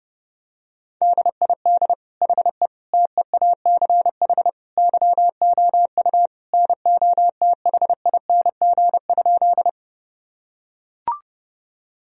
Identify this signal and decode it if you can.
Morse code, heard as a keyed tone, about 32 wpm, 700 Hz. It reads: DID HE TEACH YOU NOTHING?
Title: Sentences from Top 1000 Words - Review Part 2 - 30wpm